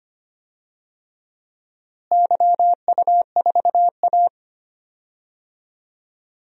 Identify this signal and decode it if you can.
Morse code, heard as a keyed tone, about 25 wpm, 700 Hz. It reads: YU4A
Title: DXCC-100 Call Signs - 4 Characters - Repeated 3x - 25wpm